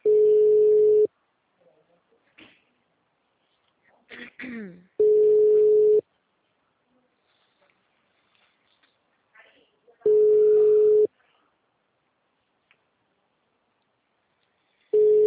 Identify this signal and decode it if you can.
Ringback tone (Europe, 425 Hz).